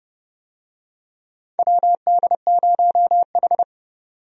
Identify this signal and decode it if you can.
Morse code, heard as a keyed tone, about 29 wpm, 700 Hz. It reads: WD0H
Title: USA Call Signs - 4 Characters - 30wpm